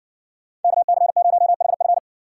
Morse code, heard as a keyed tone, about 60 wpm, 700 Hz. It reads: KC0HF